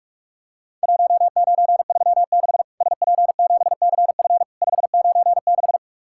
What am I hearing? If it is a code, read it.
Morse code, heard as a keyed tone, about 44 wpm, 700 Hz. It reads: 1936 SP7CF 596
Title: Worked All Europe DX Contest - QTC Exchange - 45wpm